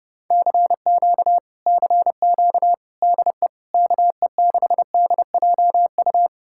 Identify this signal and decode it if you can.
Morse code, heard as a keyed tone, about 33 wpm, 700 Hz. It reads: CQ CQ DE KE6DJU